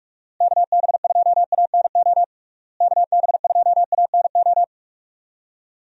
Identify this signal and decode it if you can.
Morse code, heard as a keyed tone, about 43 wpm, 700 Hz. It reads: KB2ANO KB2ANO
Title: USA Call Signs - 6 Characters - Repeated 3x - 45wpm